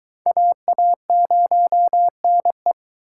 Morse code, heard as a keyed tone, about 22 wpm, 700 Hz. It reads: AA0NE